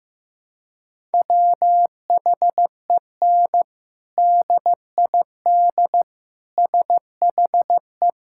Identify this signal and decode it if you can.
Morse code, heard as a keyed tone, about 15 wpm, 700 Hz. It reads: WHEN DID SHE
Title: Sentences from Top 1000 Words - Review Part 6 - 15wpm